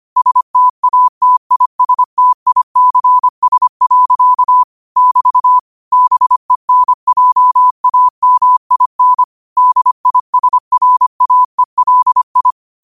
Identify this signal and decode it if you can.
Morse code, heard as a keyed tone, about 25 wpm, 1000 Hz. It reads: STATISTICS. <BT> BENJAMIN DISRAELI
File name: qotdfiles / 25wpm